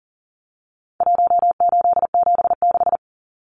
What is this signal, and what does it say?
Morse code, heard as a keyed tone, about 40 wpm, 700 Hz. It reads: 1876
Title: Numbers - Contest Style - 4-Digit 40wpm